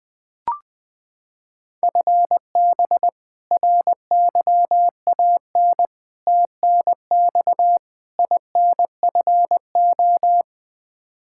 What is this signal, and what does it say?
Morse code, heard as a keyed tone, about 20 wpm, 700 Hz. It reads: FB RYAN TNX INFO
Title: Practice QSO - Exchange 3 - More Info - 20wpm